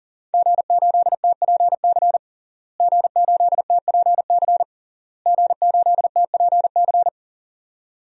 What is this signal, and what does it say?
Morse code, heard as a keyed tone, about 40 wpm, 700 Hz. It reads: G8TPC G8TPC G8TPC
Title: DXCC-100 Call Signs - 5 Characters - Repeated 3x - 40wpm